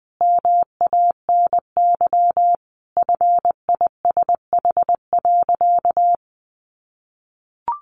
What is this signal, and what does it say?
Morse code, heard as a keyed tone, about 20 wpm, 700 Hz. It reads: MANY FISH.
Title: Sentences from Top 1500 Words 20wpm